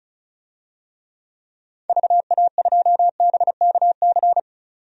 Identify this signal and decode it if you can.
Morse code, heard as a keyed tone, about 34 wpm, 700 Hz. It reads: VA2BKC
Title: DXCC-100 Call Signs - 6 Characters - 35wpm